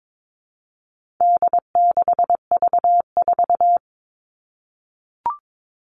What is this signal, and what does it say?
Morse code, heard as a keyed tone, about 22 wpm, 700 Hz. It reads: D6V4
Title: All Cut Numbers - 4-Digit 22wpm